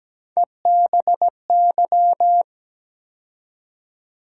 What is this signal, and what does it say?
Morse code, heard as a keyed tone, about 17 wpm, 700 Hz. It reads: EBY
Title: USA Call Sign Postfixes - 3 Letters 17wpm